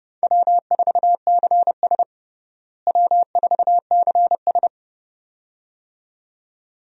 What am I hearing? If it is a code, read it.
Morse code, heard as a keyed tone, about 31 wpm, 700 Hz. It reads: W4CS W4CS
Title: USA Call Signs - 4 Characters - Repeated 2x - 30wpm